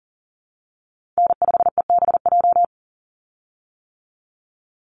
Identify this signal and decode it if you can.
Morse code, heard as a keyed tone, about 40 wpm, 700 Hz. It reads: N5EBJ
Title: USA Call Signs 40wpm